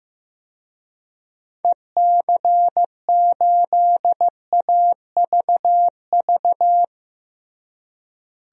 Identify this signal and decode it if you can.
Morse code, heard as a keyed tone, about 16 wpm, 700 Hz. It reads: EC8AVV
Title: DXCC-100 Call Signs - 6 Characters - Repeated 3x - 15wpm